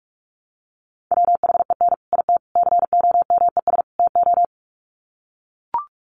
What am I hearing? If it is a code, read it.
Morse code, heard as a keyed tone, about 44 wpm, 700 Hz. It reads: WHEN IT COMES TO
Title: Sets of 4 Words 45wpm